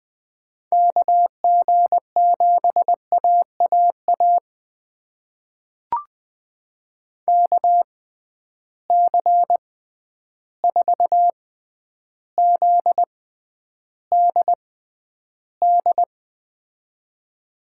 Morse code, heard as a keyed tone, about 20 wpm, spaced as for 10 wpm, 700 Hz. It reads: KG7AAA KC4ZDD
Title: USA Call Signs - 6 Characters - 6x Character Spacing - 20wpm